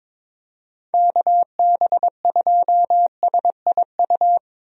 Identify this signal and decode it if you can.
Morse code, heard as a keyed tone, about 22 wpm, 700 Hz. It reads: KB2SIU
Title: USA Call Signs - 6 Characters - Repeated 2x - 22wpm